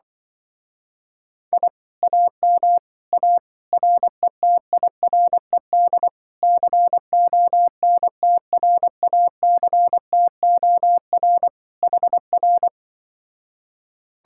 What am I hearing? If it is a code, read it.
Morse code, heard as a keyed tone, about 24 wpm, 700 Hz. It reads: I AM A RETIRED CONTRACTOR HR
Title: Speed-Racing Common QSO Phrases Encore - 20wpm